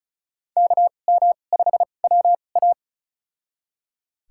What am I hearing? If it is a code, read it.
Morse code, heard as a keyed tone, about 36 wpm, spaced as for 26 wpm, 700 Hz. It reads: KM5WA
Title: USA Call Signs - 5 Characters - 2x Character Spacing - 35wpm